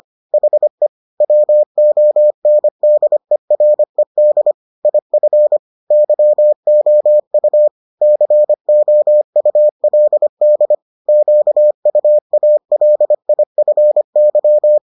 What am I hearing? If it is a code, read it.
Morse code, heard as a keyed tone, about 25 wpm, 600 Hz. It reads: HE WONDERED IF YOU COULD QUALIFY